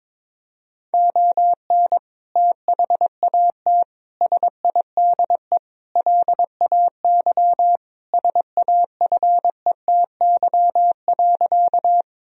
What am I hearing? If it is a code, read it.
Morse code, heard as a keyed tone, about 22 wpm, 700 Hz. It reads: ON THAT SIDE LAY SAFETY.
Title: Sentences from Top 1000 Words - Review Part 4 - 22wpm